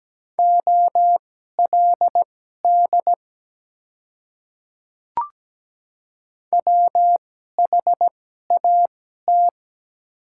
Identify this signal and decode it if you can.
Morse code, heard as a keyed tone, about 17 wpm, spaced as for 12 wpm, 700 Hz. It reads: OLD WHAT
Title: 2x Character Spacing - Top 200 Words - 17wpm